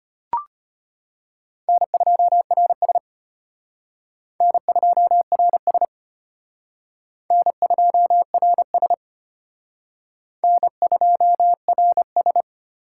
Morse code, heard as a keyed tone, about 31 wpm, 700 Hz. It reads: N2RS N2RS N2RS N2RS
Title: Speed Racing - USA 1x2 Call Signs 25wpm